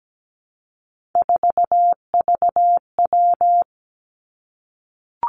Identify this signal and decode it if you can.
Morse code, heard as a keyed tone, about 17 wpm, 700 Hz. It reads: 4VW
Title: Three Challenging Letter-Number Combinations 17wpm